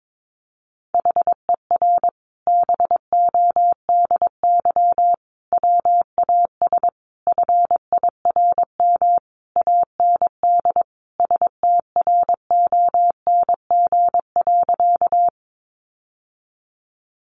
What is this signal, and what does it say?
Morse code, heard as a keyed tone, about 22 wpm, 700 Hz. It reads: HER BODY WAS FIRM AND STRONG.
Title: Sentences from Top 1400 Words 22wpm